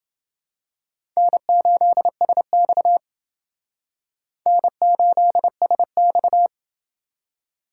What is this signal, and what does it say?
Morse code, heard as a keyed tone, about 29 wpm, 700 Hz. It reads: N8SX N8SX